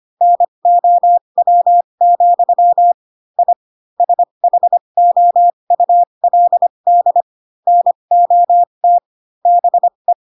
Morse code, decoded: NOW, I SHOULD NOT BE

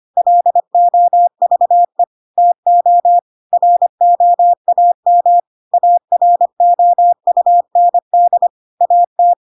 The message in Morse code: LOVE TO ROAM AROUND AT

